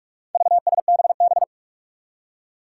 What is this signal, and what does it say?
Morse code, heard as a keyed tone, about 46 wpm, 700 Hz. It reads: VSBB
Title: Four Challenging Letter-Number Combinations 45wpm